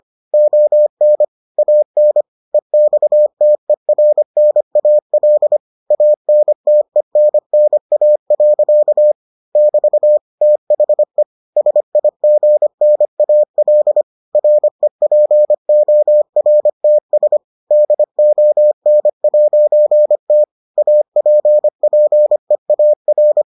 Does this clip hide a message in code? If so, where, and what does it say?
Yes: Morse code from 0.3 to 23.4 s, heard as a keyed tone, about 25 wpm, 600 Hz. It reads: ON AN EXTERNAL ANTENNA. <BT> THE SIGNAL REPORTS DON'T APPEAR